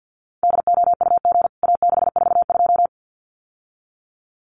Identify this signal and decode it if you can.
Morse code, heard as a keyed tone, about 49 wpm, 700 Hz. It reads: DOUG A642
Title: CWT Exchange 50wpm